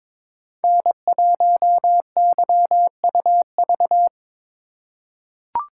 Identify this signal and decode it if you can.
Morse code, heard as a keyed tone, about 22 wpm, 700 Hz. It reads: N1YUV
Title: USA Call Signs - 5 Characters - 2x Character Spacing - 22wpm